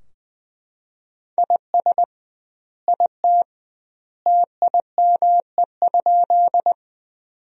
Morse code, heard as a keyed tone, about 21 wpm, 700 Hz. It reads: IS IT TIME?
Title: 2x Word-Spacing - Sentences from Top 100 Words - 20wpm